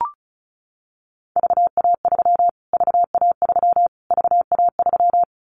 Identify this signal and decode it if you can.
Morse code, heard as a keyed tone, about 34 wpm, 700 Hz. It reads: VA3 VA3 VA3
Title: Callsign Prefixes for Top 25 DXCC - Repeated 3x - 35wpm